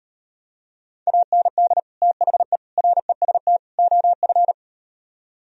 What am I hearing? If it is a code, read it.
Morse code, heard as a keyed tone, about 38 wpm, 700 Hz. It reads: AND THE REST OF